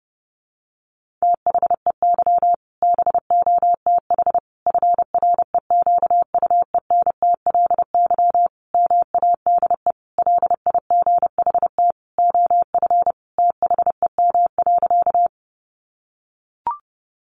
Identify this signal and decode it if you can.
Morse code, heard as a keyed tone, about 29 wpm, 700 Hz. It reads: THEY BOTH FREQUENTLY MADE LIGHT OF THEM.
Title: Sentences from Top 1300 Words 30wpm